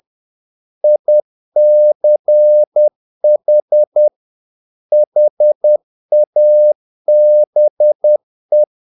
Morse code, heard as a keyed tone, about 10 wpm, 600 Hz. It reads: ICH HABE
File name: dk5ke / Humor-50BpM